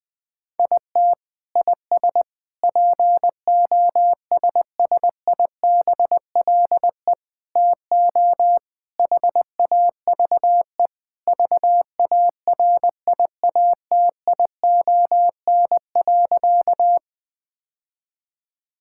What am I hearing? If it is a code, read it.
Morse code, heard as a keyed tone, about 20 wpm, 700 Hz. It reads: IT IS POSSIBLE TO HAVE VARIATION.
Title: Sentences from Top 1600 Words 20wpm